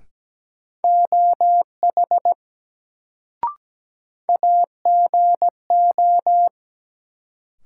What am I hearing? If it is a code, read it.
Morse code, heard as a keyed tone, about 17 wpm, 700 Hz. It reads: OH AGO
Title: Sentences from Top 500 Words - New Word Drill - 17wpm